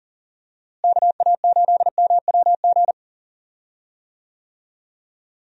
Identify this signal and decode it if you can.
Morse code, heard as a keyed tone, about 40 wpm, 700 Hz. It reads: KA8MWG
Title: USA Call Signs - 6 Characters - Repeated 2x - 40wpm